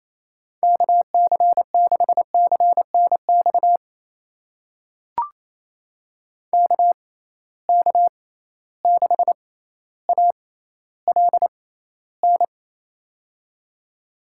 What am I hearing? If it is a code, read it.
Morse code, heard as a keyed tone, about 28 wpm, spaced as for 14 wpm, 700 Hz. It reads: KC6CNX KK6ALN
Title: USA Call Signs - 6 Characters - 6x Character Spacing - 28wpm